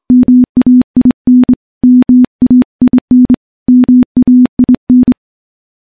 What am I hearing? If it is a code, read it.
Morse code, heard as a keyed tone, about 25 wpm, 250 Hz. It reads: MAIN MAIN MAIN
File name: morse-training-25WPM